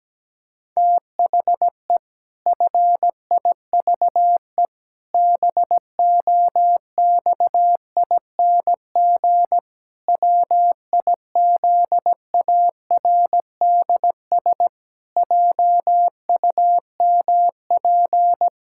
Morse code, heard as a keyed tone, about 17 wpm, 700 Hz. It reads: THE FIVE BOXING WIZARDS JUMP
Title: Pangram Challenge v2 - 17wpm